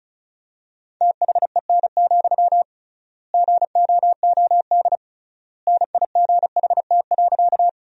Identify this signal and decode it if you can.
Morse code, heard as a keyed tone, about 36 wpm, 700 Hz. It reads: THEN, GOOD NIGHT.
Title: 3x Word-Spacing - Sentences from Top 300 Words - 35wpm